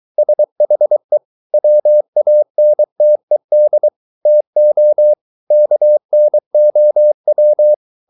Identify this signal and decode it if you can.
Morse code, heard as a keyed tone, about 23 wpm, 600 Hz. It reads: SHE WANTED TO KNOW